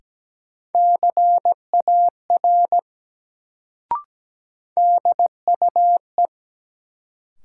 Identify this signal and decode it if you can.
Morse code, heard as a keyed tone, about 17 wpm, 700 Hz. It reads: CAR DUE
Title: Sets of 5 Words Encore - New Word Drill - 17wpm